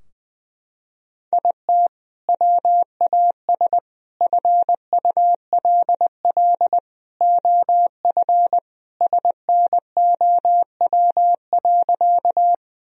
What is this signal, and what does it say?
Morse code, heard as a keyed tone, about 21 wpm, 700 Hz. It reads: IT WAS FULL OF SNOW.
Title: Sentences from Top 1000 Words - Review Part 3 - 20wpm